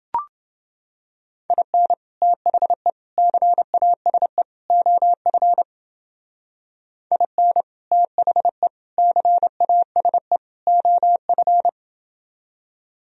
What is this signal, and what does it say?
Morse code, heard as a keyed tone, about 30 wpm, 700 Hz. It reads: IN THE CASE OF IN THE CASE OF